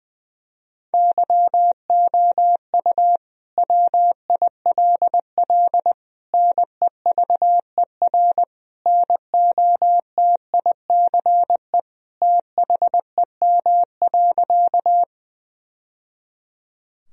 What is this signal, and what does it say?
Morse code, heard as a keyed tone, about 20 wpm, 700 Hz. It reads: YOU WILL NEVER NOTICE THEM.
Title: Sentences from Top 1000 Words - Review Part 2 - 20wpm